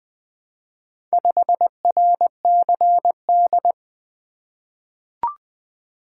Morse code, heard as a keyed tone, about 21 wpm, 700 Hz. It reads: 5RCD
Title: Four Letter-Number Combinations - 20wpm